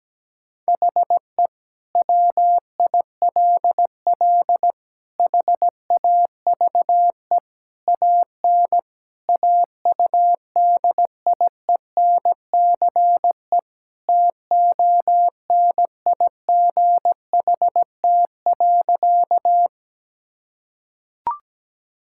Morse code, heard as a keyed tone, about 17 wpm, 700 Hz. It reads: HE WILL HAVE AN AUDIENCE TONIGHT.
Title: Sentences from Top 1000 Words - Review Part 5 - 17wpm